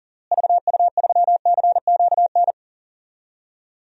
Morse code, heard as a keyed tone, about 40 wpm, 700 Hz. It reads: VU3CQN
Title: DXCC-100 Call Signs - 6 Characters - Repeated 3x - 40wpm